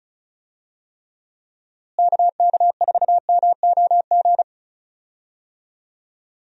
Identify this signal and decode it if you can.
Morse code, heard as a keyed tone, about 36 wpm, 700 Hz. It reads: KK4MOG
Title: USA Call Signs - 6 Characters - Repeated 3x - 35wpm